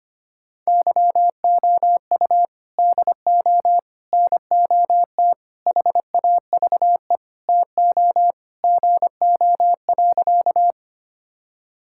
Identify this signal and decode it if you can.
Morse code, heard as a keyed tone, about 25 wpm, 700 Hz. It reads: YOU DO NOT HAVE TO GO.
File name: practice / 25wpm / Sentences with 4-Word Phrases - 25wpm